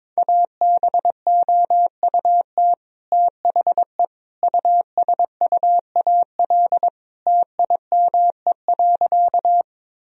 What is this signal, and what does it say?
Morse code, heard as a keyed tone, about 22 wpm, 700 Hz. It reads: ABOUT THE USUAL TIME.
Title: Sentences from Top 1000 Words - Review Part 2 - 22wpm